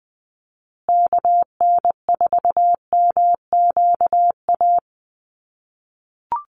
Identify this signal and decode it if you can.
Morse code, heard as a keyed tone, about 20 wpm, 700 Hz. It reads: KN4MQA
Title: USA Call Signs Repeated 3x 20wpm